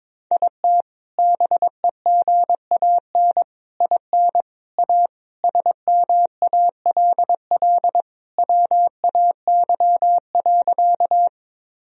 Morse code, heard as a keyed tone, about 21 wpm, 700 Hz. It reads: IT BEGAN IN A SMALL WAY.
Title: Sentences from Top 1000 Words - Review Part 1 - 22wpm